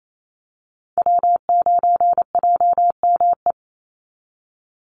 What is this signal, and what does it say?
Morse code, heard as a keyed tone, about 28 wpm, 700 Hz. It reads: W9JME
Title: USA Call Signs 28wpm